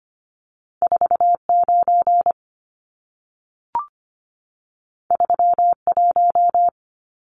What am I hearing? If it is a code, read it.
Morse code, heard as a keyed tone, about 25 wpm, 700 Hz. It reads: 49 31
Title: Numbers - 2-Digit 25wpm